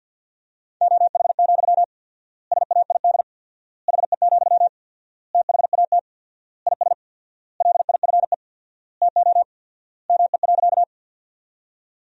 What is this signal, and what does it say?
Morse code, heard as a keyed tone, about 50 wpm, 700 Hz. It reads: OH, SAID HE, THAT IS LIFE TO ME.
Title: 4x Word-Spacing - Sentences from Top 500 Words - 50wpm